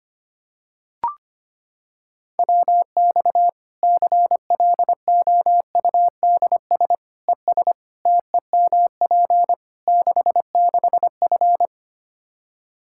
Morse code, heard as a keyed tone, about 25 wpm, 700 Hz. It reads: WX CLOUDS ES TEMP 66F
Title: Practice QSO - Exchange 2 - Info - 25wpm